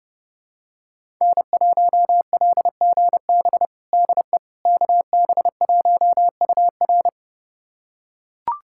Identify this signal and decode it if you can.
Morse code, heard as a keyed tone, about 30 wpm, 700 Hz. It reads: N1LGB DE KB1UR
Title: USA Callsign DE Callsign - 5 Characters - 30wpm